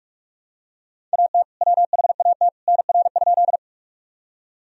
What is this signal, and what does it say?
Morse code, heard as a keyed tone, about 44 wpm, 700 Hz. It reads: AT WHAT NR?
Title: Sentences from Top 100 Words and Abbreviations 45wpm